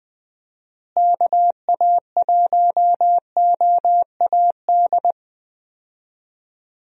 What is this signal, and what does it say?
Morse code, heard as a keyed tone, about 20 wpm, 700 Hz. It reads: KA1OAD
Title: USA Call Signs - 6 Characters - Repeated 3x - 20wpm